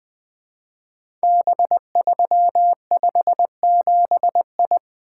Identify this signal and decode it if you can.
Morse code, heard as a keyed tone, about 20 wpm, 700 Hz. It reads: B357I
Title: Code Groups - 5 Letter:Numbers - 20wpm